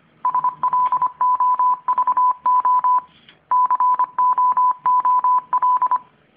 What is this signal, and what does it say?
Morse code, heard as a keyed tone, about 26 wpm, 1000 Hz. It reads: SLOVO COOL